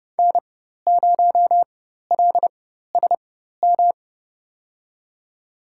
Morse code, heard as a keyed tone, about 30 wpm, spaced as for 14 wpm, 700 Hz. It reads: N0LSM
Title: USA Call Signs - 5 Characters - 4x Character Spacing - 30wpm